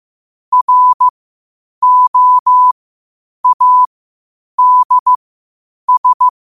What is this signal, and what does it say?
Morse code, heard as a keyed tone, about 15 wpm, spaced as for 8 wpm, 1000 Hz. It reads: ROADS
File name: qotdfiles / 8wpm